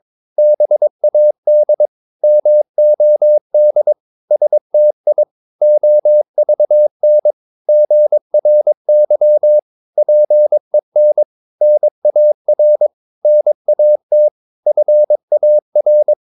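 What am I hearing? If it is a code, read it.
Morse code, heard as a keyed tone, about 22 wpm, 600 Hz. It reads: BAD MOD STI OVN GRY PEN NAR NAT FAR
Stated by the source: Morse practice of short Danish words at 22 wpm, from CW med Gnister.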